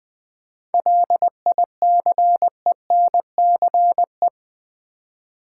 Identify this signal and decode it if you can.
Morse code, heard as a keyed tone, about 20 wpm, 700 Hz. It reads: LICENCE
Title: 7 Letter Words - First 3 Letters Given - 20wpm